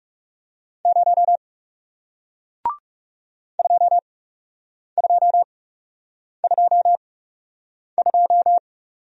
Morse code, heard as a keyed tone, about 40 wpm, spaced as for 8 wpm, 700 Hz. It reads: 0 2222